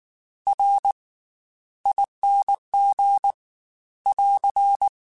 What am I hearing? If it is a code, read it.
Morse code, heard as a keyed tone, about 19 wpm, spaced as for 13 wpm, 800 Hz. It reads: RING <AR>
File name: Special INGs / Special INGs-mp3s / Slow ing @18